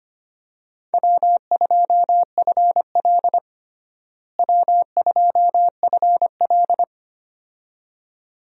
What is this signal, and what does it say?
Morse code, heard as a keyed tone, about 25 wpm, 700 Hz. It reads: W2FL W2FL
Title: USA Call Signs - 4 Characters - Repeated 2x - 25wpm